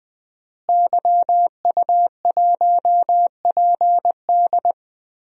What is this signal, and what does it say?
Morse code, heard as a keyed tone, about 20 wpm, 700 Hz. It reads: YU1PD
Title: DXCC-100 Call Signs - 5 Characters - Repeated 2x - 20wpm